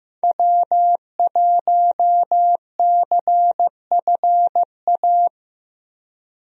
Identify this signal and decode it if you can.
Morse code, heard as a keyed tone, about 15 wpm, 700 Hz. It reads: W1CFA
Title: USA Call Signs - 5 Characters - Repeated 3x - 15wpm